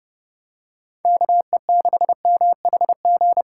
Morse code, heard as a keyed tone, about 28 wpm, 700 Hz. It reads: KE6MHG